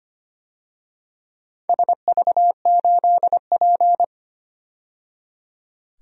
Morse code, heard as a keyed tone, about 25 wpm, 700 Hz. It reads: SV8P
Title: DXCC-100 Call Signs - 4 Characters - 25wpm